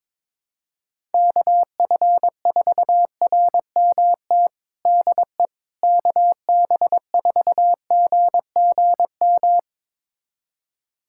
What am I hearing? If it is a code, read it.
Morse code, heard as a keyed tone, about 22 wpm, 700 Hz. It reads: KF4RMT DE KB4GGM
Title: USA Callsign DE Callsign - 6 Characters - 22wpm